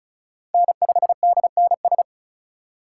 Morse code, heard as a keyed tone, about 34 wpm, 700 Hz. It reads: N5DNS
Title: USA Call Signs - 3x Character Spacing - 35wpm